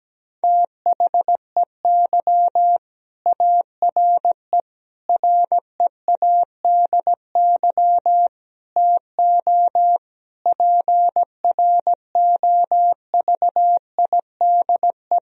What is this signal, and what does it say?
Morse code, heard as a keyed tone, about 17 wpm, 700 Hz. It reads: THEY ARE READY TO PROVIDE
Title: Sentences from Top 1000 Words - Review Part 3 - 17wpm